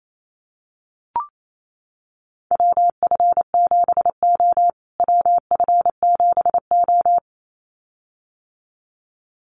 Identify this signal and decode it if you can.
Morse code, heard as a keyed tone, about 28 wpm, 700 Hz. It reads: WF7O WF7O
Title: USA Call Signs Repeated 2x 28wpm